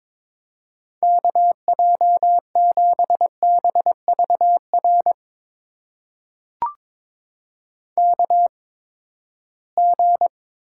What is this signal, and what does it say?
Morse code, heard as a keyed tone, about 22 wpm, spaced as for 15 wpm, 700 Hz. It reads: KJ7BVR KG
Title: USA Call Signs - 6 Characters - 8x Character Spacing - 22wpm